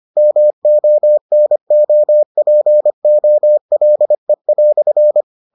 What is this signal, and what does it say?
Morse code, heard as a keyed tone, about 25 wpm, 600 Hz. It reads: MONOPOLE"